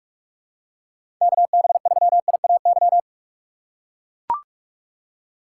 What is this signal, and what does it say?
Morse code, heard as a keyed tone, about 42 wpm, 700 Hz. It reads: KB3IAY
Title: USA Call Signs - 6 Characters - 4x Character Spacing - 45wpm